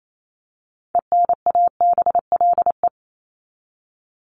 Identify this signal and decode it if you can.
Morse code, heard as a keyed tone, about 28 wpm, 700 Hz. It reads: ENABLE
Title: Top 1251-1500 Words 28wpm